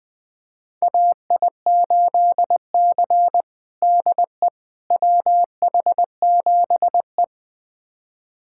Morse code, heard as a keyed tone, about 20 wpm, 700 Hz. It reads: AI8C DE WH7E
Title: USA Callsign DE Callsign - 4 Characters - 20wpm